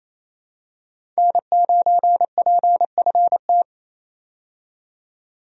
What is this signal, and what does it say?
Morse code, heard as a keyed tone, about 28 wpm, 700 Hz. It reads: N9PFT
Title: USA Call Signs - 5 Characters - 28wpm